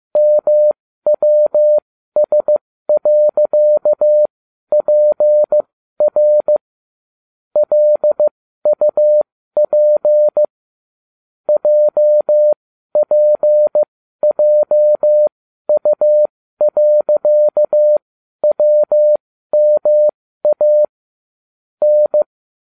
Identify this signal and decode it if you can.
Morse code, heard as a keyed tone, about 15 wpm, spaced as for 12 wpm, 600 Hz. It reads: MWS.PR LUP JPJU.WMA N